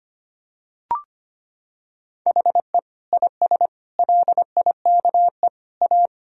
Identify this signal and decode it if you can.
Morse code, heard as a keyed tone, about 25 wpm, 700 Hz. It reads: HE IS LIKE A
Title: Sentences from Top 1000 Words - Review Part 1 - 25wpm